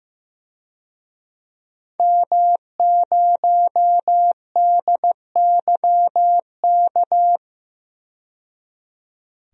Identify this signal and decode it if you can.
Morse code, heard as a keyed tone, about 15 wpm, 700 Hz. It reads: M0DYK
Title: DXCC-100 Call Signs - 5 Characters - 15wpm